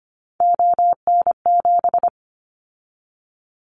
Morse code, heard as a keyed tone, about 25 wpm, 700 Hz. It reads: ON7
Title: Callsign Prefixes for Top 25 DXCC - 25wpm